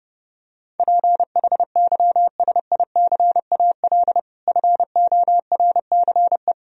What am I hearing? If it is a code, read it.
Morse code, heard as a keyed tone, about 30 wpm, 700 Hz. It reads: PHYSICAL FORCE
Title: Sentences from Top 1000 Words - Review Part 4 - 30wpm